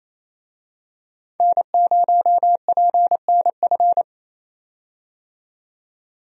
Morse code, heard as a keyed tone, about 28 wpm, 700 Hz. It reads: N0PNF
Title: USA Call Signs - 5 Characters - 28wpm